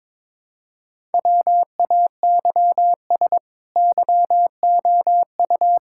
Morse code, heard as a keyed tone, about 21 wpm, 700 Hz. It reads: WAYS YOU
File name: practice / 22wpm / Sets of 8 Words - 22wpm